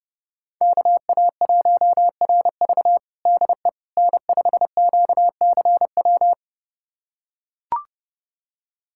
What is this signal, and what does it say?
Morse code, heard as a keyed tone, about 29 wpm, 700 Hz. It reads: KA1RV DE N5QCW
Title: USA Callsign DE Callsign - 5 Characters - 30wpm